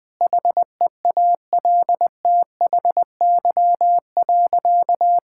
Morse code, heard as a keyed tone, about 21 wpm, 700 Hz. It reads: HEALTHY.
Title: Sentences from Top 1000 Words - Review Part 6 - 20wpm